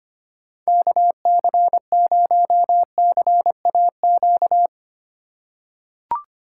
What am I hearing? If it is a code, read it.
Morse code, heard as a keyed tone, about 25 wpm, 700 Hz. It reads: KC0CAQ
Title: USA Call Signs - 6 Characters - 6x Character Spacing - 25wpm